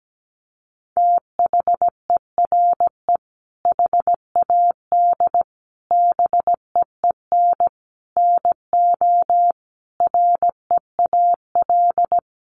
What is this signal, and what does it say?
Morse code, heard as a keyed tone, about 17 wpm, 700 Hz. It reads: THERE HAD BEEN NO REAL
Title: Sentences from Top 1400 Words 17wpm